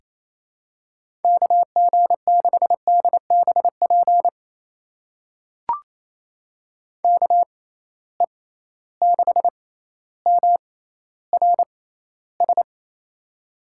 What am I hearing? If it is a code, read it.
Morse code, heard as a keyed tone, about 28 wpm, spaced as for 14 wpm, 700 Hz. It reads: KG6DBP KE6MRS
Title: USA Call Signs - 6 Characters - 6x Character Spacing - 28wpm